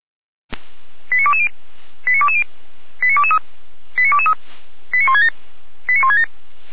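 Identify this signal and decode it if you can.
Five-tone selective call (ZVEI), heard as quick runs of five tones, tones of about 80 ms. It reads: 89220 89220 89223 89223 89167 89167
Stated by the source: ZVEI five-tone selective call received by SDR, from the audio archive of Feuerwehr Orlamünde.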